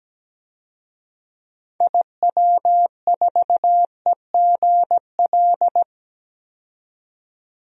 Morse code, heard as a keyed tone, about 17 wpm, 700 Hz. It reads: IW4EGL
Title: DXCC-100 Call Signs - 6 Characters - 17wpm